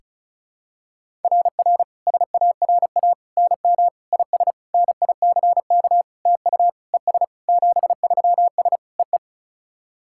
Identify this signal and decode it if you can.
Morse code, heard as a keyed tone, about 36 wpm, 700 Hz. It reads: RR SARA NM IS NICK TU ES 73S EE
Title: SOTA QSO Practice 35wpm